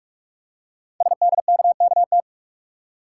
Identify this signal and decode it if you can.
Morse code, heard as a keyed tone, about 44 wpm, 700 Hz. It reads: SDKKT
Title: Code Groups - 5 Letters - 45wpm